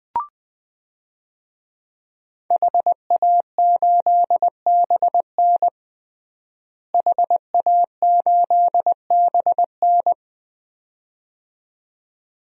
Morse code, heard as a keyed tone, about 21 wpm, 700 Hz. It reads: HA8BN HA8BN
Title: DXCC-100 Call Signs - 5 Characters - Repeated 2x - 20wpm